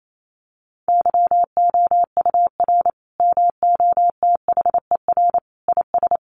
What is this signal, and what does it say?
Morse code, heard as a keyed tone, about 28 wpm, 700 Hz. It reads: YOUR MOTHER IS
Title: Sentences from Top 1100 Words 28wpm